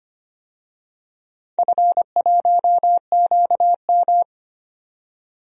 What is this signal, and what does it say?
Morse code, heard as a keyed tone, about 25 wpm, 700 Hz. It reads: F1QM
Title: DXCC-100 Call Signs - 4 Characters - 25wpm